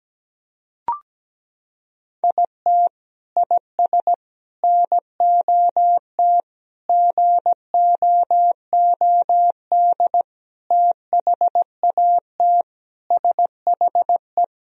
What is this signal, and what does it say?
Morse code, heard as a keyed tone, about 17 wpm, 700 Hz. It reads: IT IS NOT GOOD THAT SHE
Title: Sentences from Top 1000 Words - Review Part 4 - 17wpm